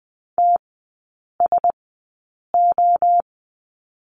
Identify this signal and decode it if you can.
Morse code, heard as a keyed tone, about 20 wpm, spaced as for 8 wpm, 700 Hz. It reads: TSO
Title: 041 Lesson 8e - Number 1 - Three Characters 20wpm